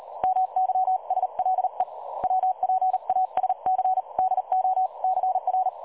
Morse code, heard as a keyed tone, about 38 wpm, 750 Hz. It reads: MYSLE WPASC DO CR
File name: dlazaawansowanych / nagraniazpasma / websdr recording 3565.0kHz